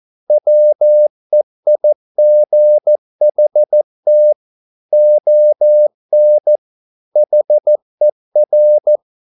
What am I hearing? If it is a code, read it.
Morse code, heard as a keyed tone, about 14 wpm, 600 Hz. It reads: WEIGHT ON HER